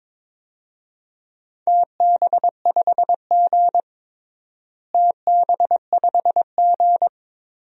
Morse code, heard as a keyed tone, about 22 wpm, 700 Hz. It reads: TB5G TB5G
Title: DXCC-100 Call Signs - 4 Characters - Repeated 2x - 22wpm